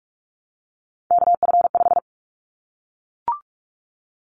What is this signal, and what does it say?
Morse code, heard as a keyed tone, about 44 wpm, 700 Hz. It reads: KF5
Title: Callsign Prefixes for Top 25 DXCC - Repeated 3x - 45wpm